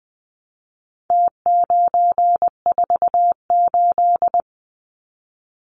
Morse code, heard as a keyed tone, about 20 wpm, 700 Hz. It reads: T948
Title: Numbers - Contest Style - 4-Digit 20wpm